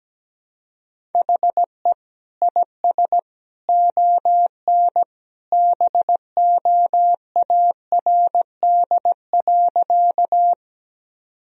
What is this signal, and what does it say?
Morse code, heard as a keyed tone, about 17 wpm, 700 Hz. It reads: HE IS ON BOARD.
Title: Sentences from Top 1000 Words - Review Part 4 - 17wpm